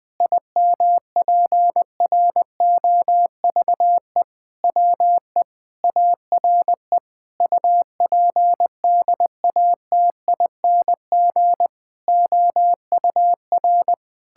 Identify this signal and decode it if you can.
Morse code, heard as a keyed tone, about 21 wpm, 700 Hz. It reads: IMPROVE WE ARE UPDATING OUR
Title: Sentences with 6-Word Phrases - 20wpm